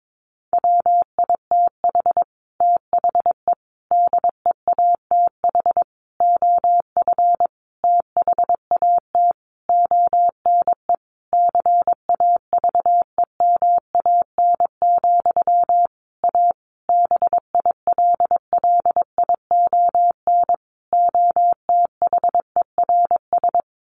Morse code, heard as a keyed tone, about 22 wpm, 700 Hz. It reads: WITH THE DEATH OF THAT ONE CAVEMAN, A BILLION OTHERS
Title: A Sound Of Thunder 22wpm